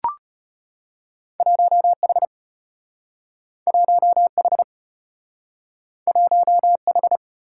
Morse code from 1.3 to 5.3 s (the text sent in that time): 1H 1H